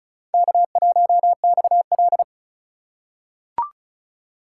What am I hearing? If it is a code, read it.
Morse code, heard as a keyed tone, about 36 wpm, 700 Hz. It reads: K1XL
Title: USA Call Signs - 4 Characters - 3x Character Spacing - 35wpm